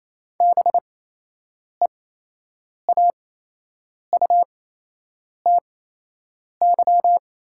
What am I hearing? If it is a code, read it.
Morse code, heard as a keyed tone, about 28 wpm, spaced as for 8 wpm, 700 Hz. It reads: BEAUTY
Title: 8x Character Spacing - Top 500 Words - 28wpm